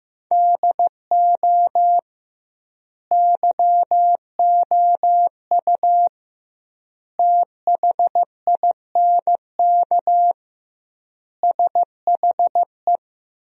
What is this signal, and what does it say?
Morse code, heard as a keyed tone, about 15 wpm, 700 Hz. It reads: DO YOU THINK SHE
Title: 2x Word-Spacing - Sentences from Top 200 Words - 15wpm